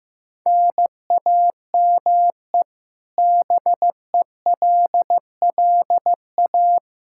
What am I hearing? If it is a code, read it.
Morse code, heard as a keyed tone, about 15 wpm, 700 Hz. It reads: NAME BELLA
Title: Practice QSO - Exchange 1 - Beginning - 15wpm